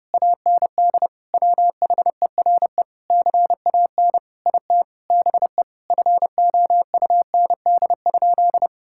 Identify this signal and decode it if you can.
Morse code, heard as a keyed tone, about 30 wpm, 700 Hz. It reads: AND WHERE CAN IT BE FOUND?
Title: Sentences from Top 1600 Words 30wpm